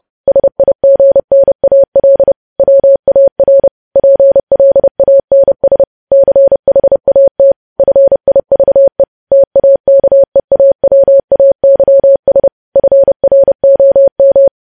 Morse code, heard as a keyed tone, about 30 wpm, 550 Hz. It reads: SIGNAL WAR PLANS CHAT FIVE TAKEAWAYS FROM
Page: news headlines in morse code at 30 wpm